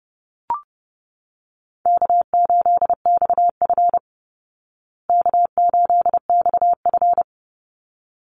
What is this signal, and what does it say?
Morse code, heard as a keyed tone, about 30 wpm, 700 Hz. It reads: K8XF K8XF
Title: Call Signs Repeated 3x - 30wpm